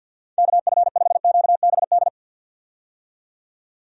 Morse code, heard as a keyed tone, about 52 wpm, 700 Hz. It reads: KF5QBD
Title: USA Call Signs - 6 Characters - 50wpm